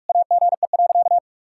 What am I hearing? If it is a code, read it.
Morse code, heard as a keyed tone, about 41 wpm, 700 Hz. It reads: AGE.
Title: Sentences from Top 1600 Words 45wpm